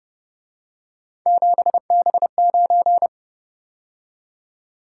Morse code, heard as a keyed tone, about 30 wpm, 700 Hz. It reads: 7B9
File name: Speed-Racing - Three Letter-Number Combinations - 25wpm